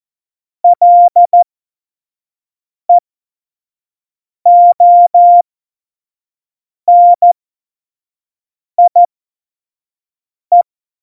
Morse code, heard as a keyed tone, about 14 wpm, spaced as for 5 wpm, 700 Hz. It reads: LEONIE
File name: Morsekurs / Witze / Witz 06. Oktober 2020, 25 F70